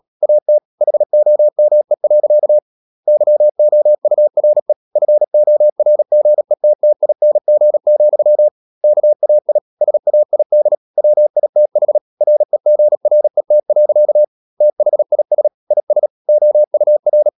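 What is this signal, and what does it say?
Morse code, heard as a keyed tone, about 37 wpm, 600 Hz. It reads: AT HOME. YOURE FORGETTING, KAI SAID WITH REGRET. THIS IS OUR
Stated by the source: Morse practice at 36 wpm, text of Ken Follett's novel Never, from CW med Gnister.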